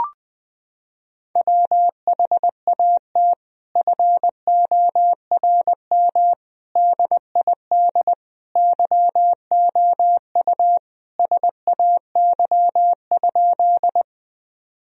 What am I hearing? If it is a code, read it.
Morse code, heard as a keyed tone, about 20 wpm, 700 Hz. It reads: WHAT FORM DID YOU SAY?
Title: Sentences from Top 1000 Words - Review Part 1 - 20wpm